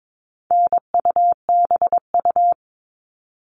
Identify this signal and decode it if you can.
Morse code, heard as a keyed tone, about 22 wpm, 700 Hz. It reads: NUBU
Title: All Cut Numbers - 4-Digit 22wpm